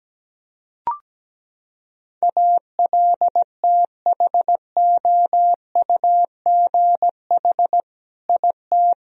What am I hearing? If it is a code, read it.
Morse code, heard as a keyed tone, about 17 wpm, 700 Hz. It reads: ALTHOUGH IT
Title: Sentences from Top 1000 Words - Review Part 3 - 17wpm